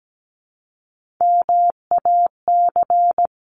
Morse code, heard as a keyed tone, about 17 wpm, 700 Hz. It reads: MAC
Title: CWT Exchange 17wpm